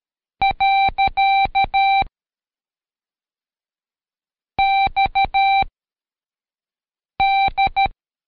Morse code, heard as a keyed tone, about 13 wpm, 780 Hz. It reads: . X D
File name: Code Quick / AudioDownld2 / Seg104-Day12-FiinalExam 1